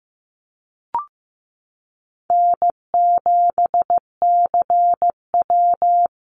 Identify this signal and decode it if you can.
Morse code, heard as a keyed tone, about 15 wpm, 700 Hz. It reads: N7CW
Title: NA SOTA Activator Callsigns 15wpm